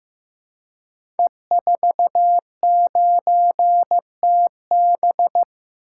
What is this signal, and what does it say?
Morse code, heard as a keyed tone, about 15 wpm, 700 Hz. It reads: E49TB
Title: Code Groups - 5 Letter:Numbers - 15wpm